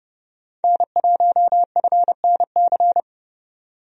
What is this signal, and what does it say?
Morse code, heard as a keyed tone, about 30 wpm, 700 Hz. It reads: N1FNC
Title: USA Call Signs - 5 Characters - 30wpm